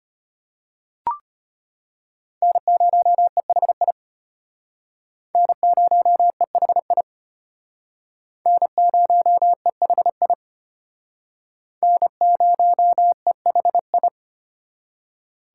Morse code, heard as a keyed tone, about 31 wpm, 700 Hz. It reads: N0EHI N0EHI N0EHI N0EHI
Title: Speed Racing - USA 1x3 Call Signs 25wpm